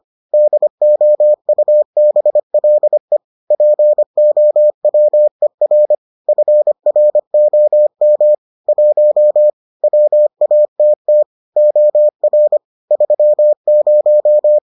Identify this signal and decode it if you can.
Morse code, heard as a keyed tone, about 25 wpm, 600 Hz. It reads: DOUBLE POWER FROM 1 WATT OR 30